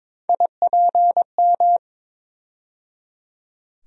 Morse code, heard as a keyed tone, about 21 wpm, 700 Hz. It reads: IPM
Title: USA Call Sign Postfixes - 3 Letters 22wpm